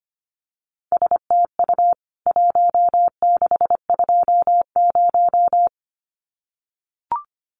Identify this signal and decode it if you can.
Morse code, heard as a keyed tone, about 25 wpm, 700 Hz. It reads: STU 1620
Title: CWT Exchange 25wpm